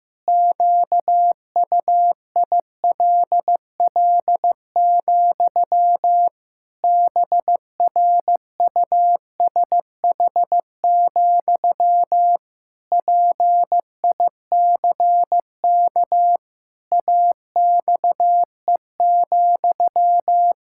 Morse code, heard as a keyed tone, about 15 wpm, 700 Hz. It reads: QUILL, BRUSH, PICK AXE,
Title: Pangram Challenge v2 - 15wpm